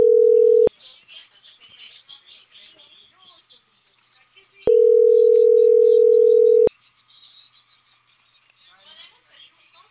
Ringback tone (North America), 440 + 480 Hz.